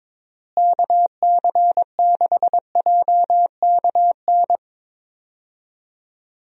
Morse code, heard as a keyed tone, about 22 wpm, 700 Hz. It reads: KC6JKN
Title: USA Call Signs - 6 Characters - Repeated 2x - 22wpm